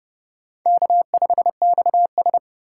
Morse code, heard as a keyed tone, about 31 wpm, 700 Hz. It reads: K5XS
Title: USA Call Signs - 4 Characters - 6x Character Spacing - 30wpm